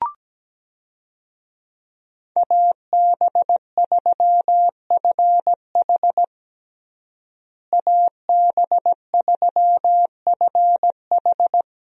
Morse code, heard as a keyed tone, about 17 wpm, 700 Hz. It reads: AB3FH AB3FH
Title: USA Call Signs - 5 Characters - Repeated 2x - 17wpm